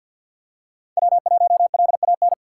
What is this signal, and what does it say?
Morse code, heard as a keyed tone, about 52 wpm, 700 Hz. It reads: W1LAN
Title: USA Call Signs - 5 Characters - 4x Character Spacing - 50wpm